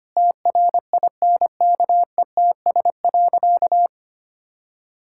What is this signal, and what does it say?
Morse code, heard as a keyed tone, about 25 wpm, 700 Hz. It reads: TRINKETS.
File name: practice / 25wpm / Sentences with 2-Word Phrases - 25wpm